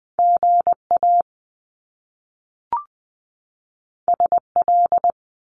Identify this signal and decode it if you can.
Morse code, heard as a keyed tone, about 20 wpm, 700 Hz. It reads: GA SL
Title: Top Two Letter Combinations in Words 20wpm